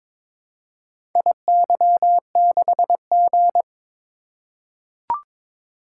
Morse code, heard as a keyed tone, about 22 wpm, 700 Hz. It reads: IY6G
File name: practice / 22wpm / Four Letter-Number Combinations - 22wpm